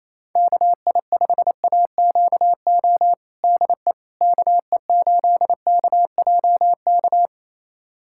Morse code, heard as a keyed tone, about 28 wpm, 700 Hz. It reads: KI5AQO DE KE8KJK
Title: USA Callsign DE Callsign 28wpm